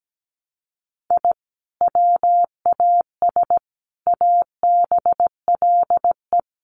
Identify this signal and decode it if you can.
Morse code, heard as a keyed tone, about 17 wpm, 700 Hz. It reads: I WAS ABLE
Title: Sets of 4 Words 17wpm